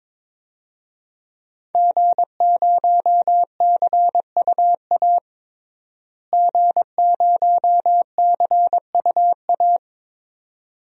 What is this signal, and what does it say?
Morse code, heard as a keyed tone, about 22 wpm, 700 Hz. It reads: G0CUA G0CUA
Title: DXCC-100 Call Signs - 5 Characters - Repeated 3x - 22wpm